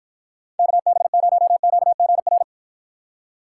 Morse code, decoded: KB0QGR

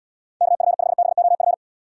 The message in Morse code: KF5BOF

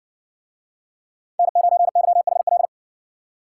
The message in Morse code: N9YHL

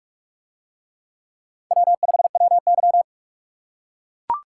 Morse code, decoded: W5WY

